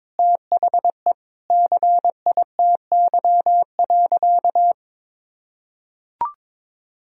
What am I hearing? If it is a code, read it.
Morse code, heard as a keyed tone, about 21 wpm, 700 Hz. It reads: THE CITY.
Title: Sentences from Top 1000 Words - Review Part 2 - 22wpm